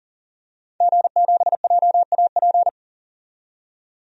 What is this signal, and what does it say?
Morse code, heard as a keyed tone, about 40 wpm, 700 Hz. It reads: G7JAP
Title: DXCC-100 Call Signs - 5 Characters - 40wpm